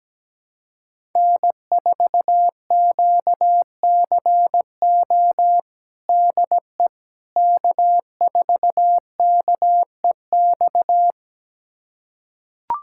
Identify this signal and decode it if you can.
Morse code, heard as a keyed tone, about 17 wpm, 700 Hz. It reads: N4QCO DE K4KEX
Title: USA Callsign DE Callsign 17wpm